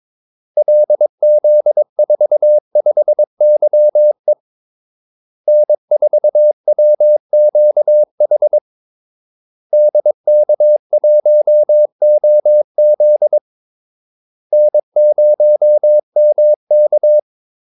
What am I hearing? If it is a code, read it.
Morse code, heard as a keyed tone, about 22 wpm, 600 Hz. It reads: LZ45YE N4WQH DK1OZ N0MK
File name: Callsigns-22wpm